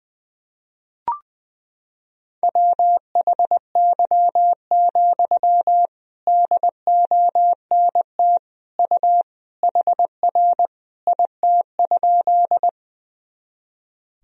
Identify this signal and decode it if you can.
Morse code, heard as a keyed tone, about 20 wpm, 700 Hz. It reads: WHY, DONT U HR IT?
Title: Sentences from Top 300 Words and Abbreviations 20wpm